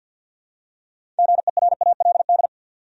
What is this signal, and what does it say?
Morse code, heard as a keyed tone, about 47 wpm, 700 Hz. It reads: GERALD
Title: USA Common Names - 50wpm